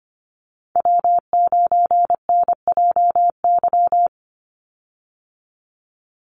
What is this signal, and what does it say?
Morse code, heard as a keyed tone, about 25 wpm, 700 Hz. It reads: W9NJY
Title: Call Signs - 25wpm